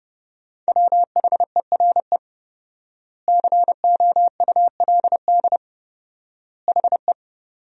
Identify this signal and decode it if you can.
Morse code, heard as a keyed tone, about 32 wpm, 700 Hz. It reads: WHERE COULD HE
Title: 4x Word-Spacing - Sentences from Top 200 Words - 30wpm